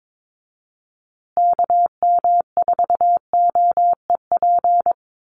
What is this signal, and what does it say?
Morse code, heard as a keyed tone, about 22 wpm, 700 Hz. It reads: KM4OEP
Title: USA Call Signs Repeated 3x 22wpm